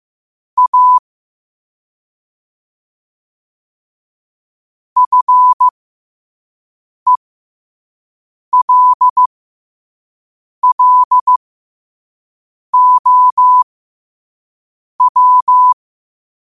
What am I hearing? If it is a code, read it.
Morse code, heard as a keyed tone, about 15 wpm, spaced as for 5 wpm, 1000 Hz. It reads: A FELLOW
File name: qotdfiles / 5wpm